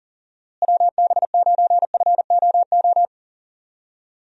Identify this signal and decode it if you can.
Morse code, heard as a keyed tone, about 40 wpm, 700 Hz. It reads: WB9FOO
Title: USA Call Signs - 6 Characters - Repeated 3x - 40wpm